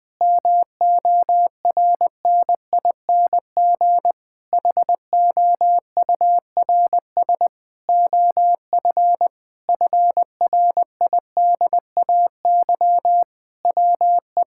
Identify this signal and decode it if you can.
Morse code, heard as a keyed tone, about 20 wpm, 700 Hz. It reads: MORNING HOURS OF FRIDAY WE
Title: Sentences with 6-Word Phrases - 20wpm